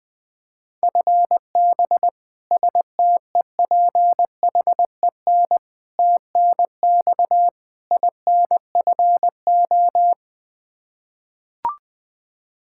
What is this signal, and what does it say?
Morse code, heard as a keyed tone, about 20 wpm, 700 Hz. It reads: FB STEPHEN TNX INFO
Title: Practice QSO - Exchange 3 - More Info - 20wpm